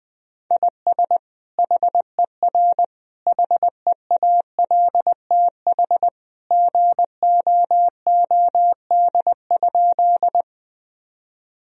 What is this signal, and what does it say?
Morse code, heard as a keyed tone, about 20 wpm, 700 Hz. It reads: IS HER HEALTH GOOD?
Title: Sentences from Top 1000 Words - Review Part 3 - 20wpm